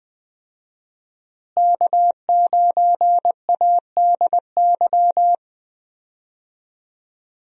Morse code, heard as a keyed tone, about 20 wpm, 700 Hz. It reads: K9ADY
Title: USA Call Signs - 5 Characters - Repeated 3x - 20wpm